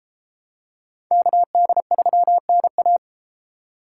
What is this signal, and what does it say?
Morse code, heard as a keyed tone, about 32 wpm, 700 Hz. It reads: KD3NA